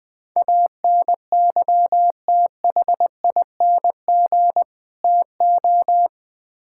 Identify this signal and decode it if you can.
Morse code, heard as a keyed tone, about 20 wpm, 700 Hz. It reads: ANYTHING TO
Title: Build A Phrase - 4-Word Phrases - 20wpm